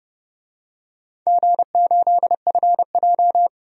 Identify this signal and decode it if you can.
Morse code, heard as a keyed tone, about 30 wpm, 700 Hz. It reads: G8FJ
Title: DXCC-100 Call Signs - 4 Characters - 30wpm